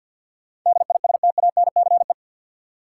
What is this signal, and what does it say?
Morse code, heard as a keyed tone, about 53 wpm, 700 Hz. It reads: DISTANCE
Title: Sentences from Top 1200 Words - New Word Drill Alt - 50wpm